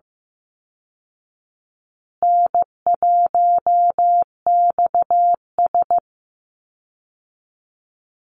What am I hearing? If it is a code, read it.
Morse code, heard as a keyed tone, about 15 wpm, 700 Hz. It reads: N1XS
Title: Easy Call Signs Repeated 3x - 15wpm